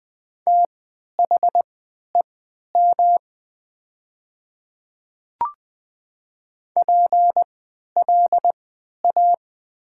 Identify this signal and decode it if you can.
Morse code, heard as a keyed tone, about 20 wpm, spaced as for 11 wpm, 700 Hz. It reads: THEM PLA
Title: 3x Character Spacing - Top 300 Words - 20wpm